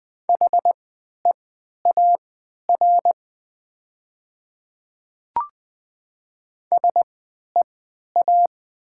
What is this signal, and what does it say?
Morse code, heard as a keyed tone, about 20 wpm, spaced as for 11 wpm, 700 Hz. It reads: HEAR SEA
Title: 3x Character Spacing - Top 300 Words - 20wpm